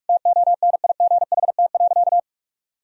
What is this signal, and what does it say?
Morse code, heard as a keyed tone, about 45 wpm, 700 Hz. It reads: TONIGHT.
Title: Sentences from Top 1000 Words - Review Part 4 - 45wpm